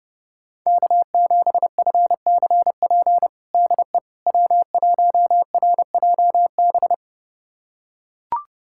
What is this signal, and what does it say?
Morse code, heard as a keyed tone, about 30 wpm, 700 Hz. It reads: K7FCP DE W1RJB
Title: USA Callsign DE Callsign - 5 Characters - 30wpm